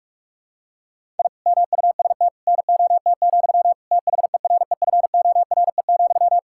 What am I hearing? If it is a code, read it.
Morse code, heard as a keyed tone, about 46 wpm, 700 Hz. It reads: I MUST NOT, THEREFORE,
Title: Sentences from Top 1000 Words - Review Part 5 - 45wpm